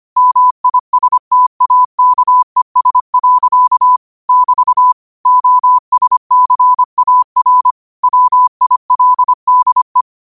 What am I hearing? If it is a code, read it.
Morse code, heard as a keyed tone, about 25 wpm, 1000 Hz. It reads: MISTAKES. <BT> OSCAR WILDE
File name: qotdfiles / 25wpm